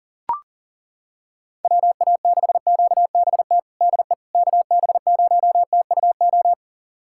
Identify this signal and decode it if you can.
Morse code, heard as a keyed tone, about 40 wpm, 700 Hz. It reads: WA6QBT DE KB0TUO
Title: USA Callsign DE Callsign 40wpm